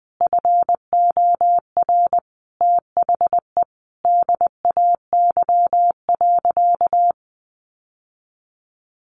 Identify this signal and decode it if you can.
Morse code, heard as a keyed tone, about 20 wpm, 700 Hz. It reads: FOR THE DAY.
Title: Sentences from Top 1100 Words 20wpm